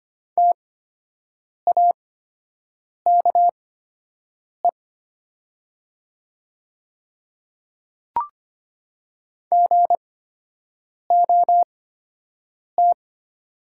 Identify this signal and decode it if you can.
Morse code, heard as a keyed tone, about 25 wpm, spaced as for 7 wpm, 700 Hz. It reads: TAKE GOT
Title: 8x Character Spacing - Top 300 Words - 25wpm